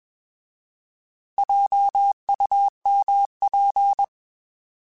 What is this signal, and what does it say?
Morse code, heard as a keyed tone, about 21 wpm, 800 Hz. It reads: JUMP
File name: Alphabetical / Alphabetical mp3s / Js @20